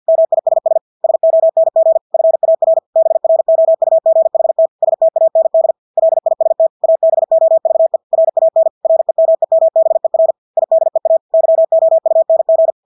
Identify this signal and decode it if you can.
Morse code, heard as a keyed tone, about 50 wpm, 650 Hz. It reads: MISS SONG FAR BROUGHT STAND LIST ABOVE RAN REMEMBER IDEA YOUNG
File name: Engelske ord 50 wpm